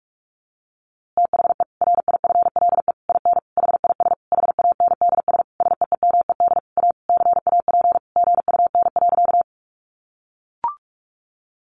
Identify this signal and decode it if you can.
Morse code, heard as a keyed tone, about 44 wpm, 700 Hz. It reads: THE RIFLE IN HIS HANDS SEEMED A CAP GUN.
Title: A Sound Of Thunder 45wpm